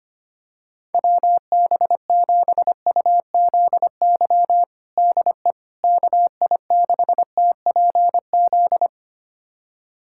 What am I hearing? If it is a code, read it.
Morse code, heard as a keyed tone, about 25 wpm, 700 Hz. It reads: WB7UZY DE KI6TPZ
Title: USA Callsign DE Callsign - 6 Characters - 25wpm